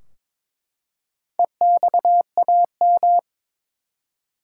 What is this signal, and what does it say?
Morse code, heard as a keyed tone, about 22 wpm, 700 Hz. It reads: EXAM
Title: Four Letter Words - 22wpm